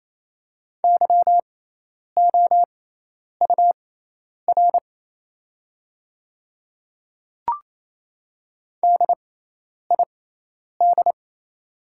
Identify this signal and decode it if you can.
Morse code, heard as a keyed tone, about 27 wpm, spaced as for 10 wpm, 690 Hz. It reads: YOUR DID
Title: 6x Character Spacing - Top 100 Words - 28wpm